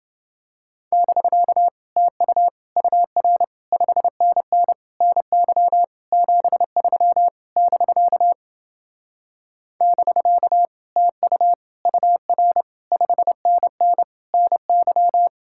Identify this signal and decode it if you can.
Morse code, heard as a keyed tone, about 30 wpm, 700 Hz. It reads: <BK> TU UR 5NN NY 73 <BK> <BK> TU UR 5NN NY